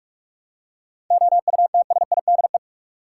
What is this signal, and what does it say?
Morse code, heard as a keyed tone, about 46 wpm, 700 Hz. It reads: OUTSIDE
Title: Seven Letter Words - 45wpm